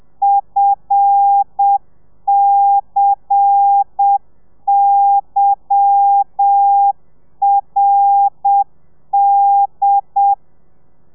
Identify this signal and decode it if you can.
Morse code, heard as a keyed tone, about 7 wpm, 800 Hz. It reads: FCYRD
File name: telegrafia / i2tfj / Lez10 30